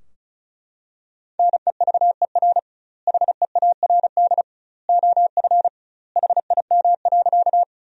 Morse code, heard as a keyed tone, about 36 wpm, 700 Hz. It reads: NEVER HEARD OF HIM.
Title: 2x Word-Spacing - Sentences from Top 400 Words - 35wpm